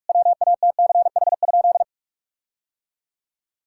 Morse code, heard as a keyed tone, about 46 wpm, 700 Hz. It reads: WATCH?
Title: Sentences from Top 1600 Words 45wpm